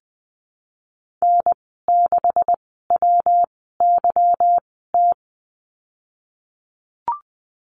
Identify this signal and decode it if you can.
Morse code, heard as a keyed tone, about 19 wpm, spaced as for 14 wpm, 700 Hz. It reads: N6WYT
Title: 179 Lesson 29g - Slash - Call Signs 20wpm